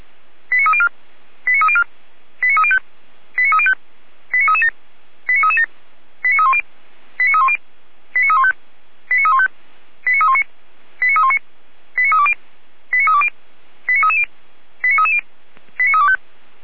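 Five-tone selective call (ZVEI), heard as quick runs of five tones, tones of about 70 ms. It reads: 89304 89304 89305 89305 89337 89337 89310 89310 89315 89315 89319 89319 89320 89320 89330 89330 89325
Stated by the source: ZVEI five-tone call received by SDR.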